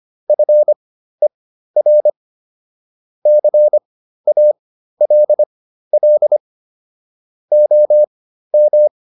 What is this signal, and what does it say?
Morse code, heard as a keyed tone, about 25 wpm, spaced as for 13 wpm, 600 Hz. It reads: FER CALL OM
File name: QSO409 13